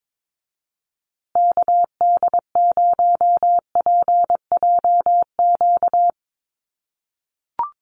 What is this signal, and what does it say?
Morse code, heard as a keyed tone, about 22 wpm, 700 Hz. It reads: KD0PJQ
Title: USA Call Signs Repeated 2x 22wpm